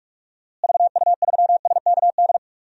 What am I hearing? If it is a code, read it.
Morse code, heard as a keyed tone, about 46 wpm, 700 Hz. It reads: VU3SKD
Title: DXCC-100 Call Signs - 6 Characters - Repeated 2x - 45wpm